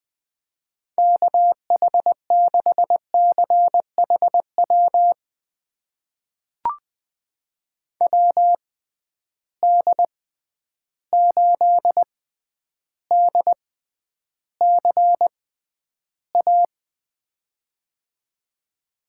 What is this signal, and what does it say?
Morse code, heard as a keyed tone, about 20 wpm, spaced as for 10 wpm, 700 Hz. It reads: KH6CHW WD8DCA
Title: USA Call Signs - 6 Characters - 6x Character Spacing - 20wpm